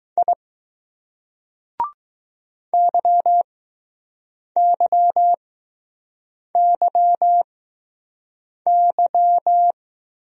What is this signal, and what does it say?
Morse code, heard as a keyed tone, about 20 wpm, spaced as for 6 wpm, 700 Hz. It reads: I YYYY